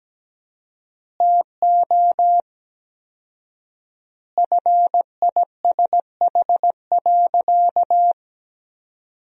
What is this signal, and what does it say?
Morse code, heard as a keyed tone, about 17 wpm, 700 Hz. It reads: TO FISH.
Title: 4x Word-Spacing - Sentences from Top 300 Words - 17wpm